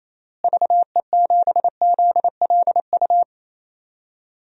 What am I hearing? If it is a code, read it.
Morse code, heard as a keyed tone, about 28 wpm, 700 Hz. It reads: VE7ZLU
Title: DXCC-100 Call Signs - 6 Characters - Repeated 3x - 28wpm